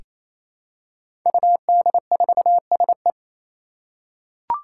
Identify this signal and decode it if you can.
Morse code, heard as a keyed tone, about 27 wpm, 700 Hz. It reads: UD4SE
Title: Five Challenging Letter-Number Combinations 28wpm